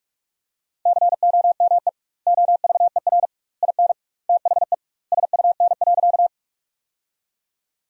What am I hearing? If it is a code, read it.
Morse code, heard as a keyed tone, about 48 wpm, 700 Hz. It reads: COME OVER IN THE SUN.
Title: 2x Word-Spacing - Sentences from Top 300 Words - 45wpm